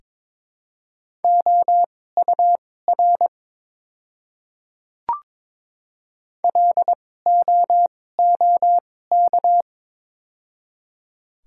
Morse code, heard as a keyed tone, about 22 wpm, spaced as for 16 wpm, 700 Hz. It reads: OUR LOOK
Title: 2x Character Spacing - Top 200 Words - 22wpm